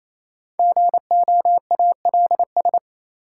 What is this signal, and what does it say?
Morse code, heard as a keyed tone, about 28 wpm, 700 Hz. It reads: GOALS
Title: Sentences from Top 1000 Words - Review Part 6 - New Word Drill - 28wpm